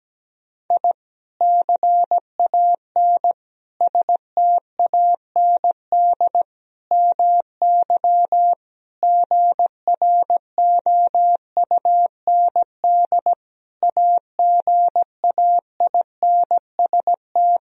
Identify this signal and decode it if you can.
Morse code, heard as a keyed tone, about 17 wpm, 700 Hz. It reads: I CAN STAND MY GROUND AGAINST
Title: Sentences from Top 1000 Words - Review Part 2 - 17wpm